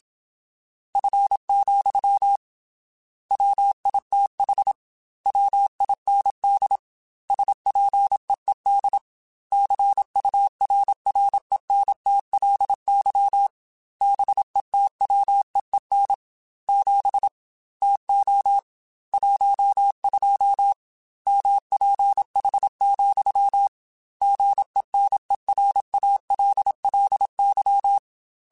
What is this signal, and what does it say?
Morse code, heard as a keyed tone, about 27 wpm, 800 Hz. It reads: F, WITH WIND SPEED CURRENTLY BETWEEN 7 TO 12 MPH, GENERALLY
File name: Weather Sets / Weather Sets-mp3s / Weather 5 22 @25